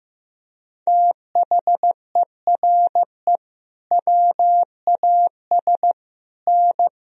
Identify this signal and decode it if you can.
Morse code, heard as a keyed tone, about 15 wpm, 700 Hz. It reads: THERE WAS N
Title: Sentences from Top 300 Words and Abbreviations 15wpm